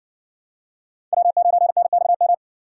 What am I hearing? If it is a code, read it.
Morse code, heard as a keyed tone, about 58 wpm, 700 Hz. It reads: W9NXM